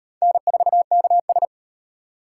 Morse code, heard as a keyed tone, about 34 wpm, 700 Hz. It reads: N4KS